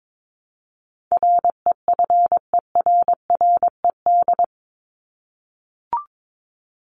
Morse code, heard as a keyed tone, about 22 wpm, 700 Hz. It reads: REFERRED
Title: ED Suffix - Top 100 Words with Base Words 22wpm